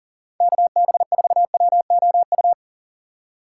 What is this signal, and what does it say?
Morse code, heard as a keyed tone, about 40 wpm, 700 Hz. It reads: KB4WOU
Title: USA Call Signs - 6 Characters - 6x Character Spacing - 40wpm